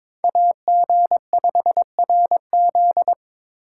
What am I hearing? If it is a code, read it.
Morse code, heard as a keyed tone, about 27 wpm, 700 Hz. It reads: AG5RZ
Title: Speed Racing - USA 2x2 Call Signs 22wpm